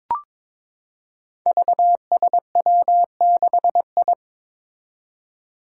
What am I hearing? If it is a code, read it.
Morse code, heard as a keyed tone, about 23 wpm, 700 Hz. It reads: VSW6I
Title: Five Challenging Letter-Number Combinations 22wpm